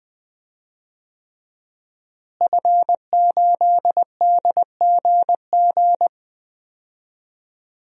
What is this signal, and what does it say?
Morse code, heard as a keyed tone, about 20 wpm, 700 Hz. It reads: F8DGG
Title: DXCC-100 Call Signs - 5 Characters - Repeated 3x - 20wpm